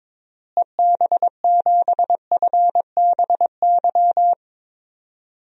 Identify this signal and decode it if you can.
Morse code, heard as a keyed tone, about 22 wpm, 700 Hz. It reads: EB7FBY
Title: DXCC-100 Call Signs - 6 Characters - Repeated 3x - 22wpm